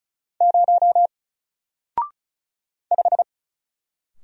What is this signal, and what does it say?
Morse code, heard as a keyed tone, about 34 wpm, 700 Hz. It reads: 0 5
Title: Single Letter-Number Alt 35wpm